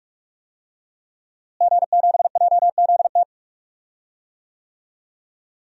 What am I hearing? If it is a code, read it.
Morse code, heard as a keyed tone, about 41 wpm, 700 Hz. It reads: G7JZT
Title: DXCC-100 Call Signs - 5 Characters - Repeated 2x - 45wpm